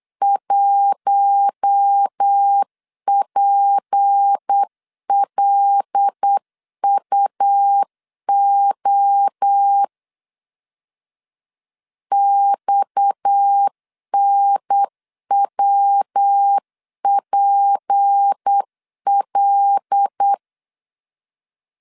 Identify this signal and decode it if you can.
Morse code, heard as a keyed tone, about 8 wpm, 800 Hz. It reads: 1PLUO XNWPL
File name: CW / DET / 1 6 1500C S C 8wpm 5219